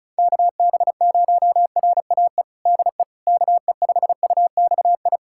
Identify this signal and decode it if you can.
Morse code, heard as a keyed tone, about 36 wpm, 700 Hz. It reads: KB0RAE DE KE5UXI
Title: USA Callsign DE Callsign 35wpm